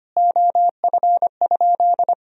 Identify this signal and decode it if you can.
Morse code, heard as a keyed tone, about 25 wpm, 700 Hz. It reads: OF?
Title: Sentences from Top 200 Words and Abbreviations 25wpm